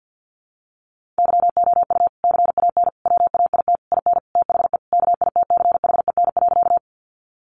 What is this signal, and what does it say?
Morse code, heard as a keyed tone, about 48 wpm, 700 Hz. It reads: YOU CAN WAIT IN THE KITCHEN.
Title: Sentences from Top 1400 Words 50wpm